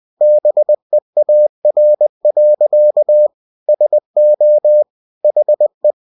Morse code, heard as a keyed tone, about 20 wpm, 600 Hz. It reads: BEAR. SO HE
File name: cw-books / winnie-the-pooh / winnie-20wpm-page-0075